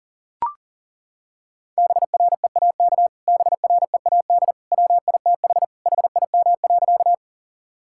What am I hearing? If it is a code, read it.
Morse code, heard as a keyed tone, about 40 wpm, 700 Hz. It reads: BREAK BREAD WITH HIM.
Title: Sentences from Top 1600 Words 40wpm